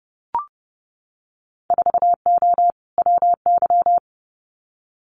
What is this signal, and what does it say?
Morse code, heard as a keyed tone, about 30 wpm, 700 Hz. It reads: 4O WY
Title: Winter Field Day Exchanges Encore 30wpm